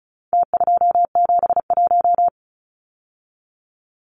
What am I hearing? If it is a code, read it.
Morse code, heard as a keyed tone, about 34 wpm, 700 Hz. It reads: T271
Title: Numbers - Contest Style - 4-Digit 35wpm